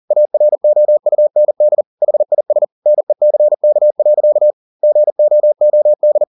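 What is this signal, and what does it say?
Morse code, heard as a keyed tone, about 40 wpm, 600 Hz. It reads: AROUND HIS NECK. GOOD
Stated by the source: Morse code at 40 wpm, practice text, CW med Gnister.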